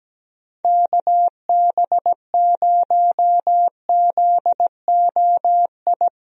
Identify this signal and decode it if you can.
Morse code, heard as a keyed tone, about 17 wpm, 700 Hz. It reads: KB0ZOI
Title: USA Call Signs - 6 Characters - Repeated 2x - 17wpm